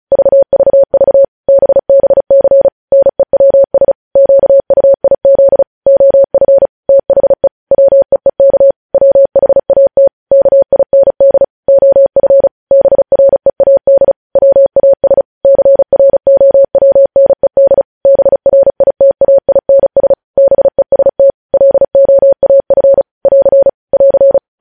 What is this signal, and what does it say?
Morse code, heard as a keyed tone, about 36 wpm, 550 Hz. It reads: VVV BBC NEWS QUIZ OF THE WEEK WHAT KIND OF BREAD WAS CROWNED BRITAINS BEST LOAF <AR> <AR>
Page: News Headlines in Morse Code at 35 WPM